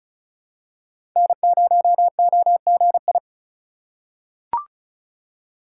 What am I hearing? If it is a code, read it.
Morse code, heard as a keyed tone, about 36 wpm, 700 Hz. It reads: N0OGI
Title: USA Call Signs - 5 Characters - 3x Character Spacing - 35wpm